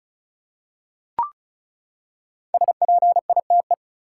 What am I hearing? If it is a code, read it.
Morse code, heard as a keyed tone, about 36 wpm, 700 Hz. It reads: SPITE
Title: Five Letter Words - 35wpm